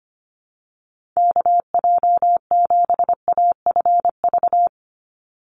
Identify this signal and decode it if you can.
Morse code, heard as a keyed tone, about 25 wpm, 700 Hz. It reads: KJ7AFV
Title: USA Call Signs 25wpm